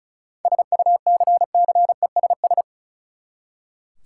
Morse code, heard as a keyed tone, about 36 wpm, 700 Hz. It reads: SUCCESS
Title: Seven Letter Words - 35wpm